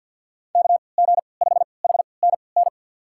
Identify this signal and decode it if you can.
Morse code, heard as a keyed tone, about 51 wpm, spaced as for 28 wpm, 700 Hz. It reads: KG5HNN
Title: USA Call Signs - 3x Character Spacing - 50wpm